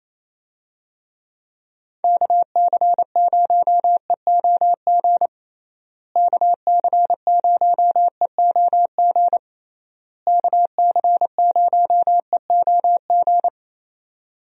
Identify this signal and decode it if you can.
Morse code, heard as a keyed tone, about 28 wpm, 700 Hz. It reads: KC0EOG KC0EOG KC0EOG
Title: USA Call Signs - 6 Characters - Repeated 3x - 28wpm